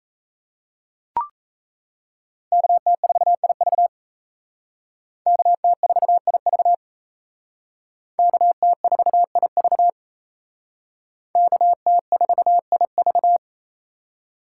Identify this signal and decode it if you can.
Morse code, heard as a keyed tone, about 35 wpm, 700 Hz. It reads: KT4IV KT4IV KT4IV KT4IV